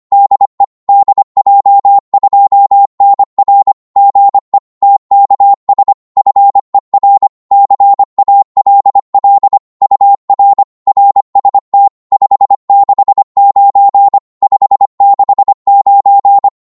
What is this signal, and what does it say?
Morse code, heard as a keyed tone, about 25 wpm, 800 Hz. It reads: DE DJ2NR GE TKS FER CALL UR RST 569 569
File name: Morse / qso-25wpm